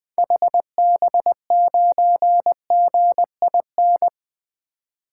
Morse code, heard as a keyed tone, about 21 wpm, 700 Hz. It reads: HB9GIN
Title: 2023 SOTA Activator Call Signs - 20wpm